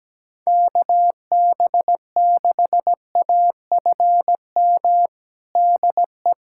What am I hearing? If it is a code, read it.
Morse code, heard as a keyed tone, about 17 wpm, 700 Hz. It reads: KB6AFM DE
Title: USA Callsign DE Callsign - 6 Characters - 17wpm